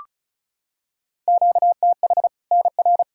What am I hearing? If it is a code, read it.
Morse code, heard as a keyed tone, about 36 wpm, 700 Hz. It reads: QTH NR
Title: Practice QSO - Exchange 1 - Beginning - 35wpm